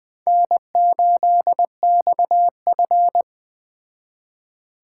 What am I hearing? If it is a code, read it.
Morse code, heard as a keyed tone, about 20 wpm, 700 Hz. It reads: N8XF
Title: Speed Racing - USA Call Signs 17wpm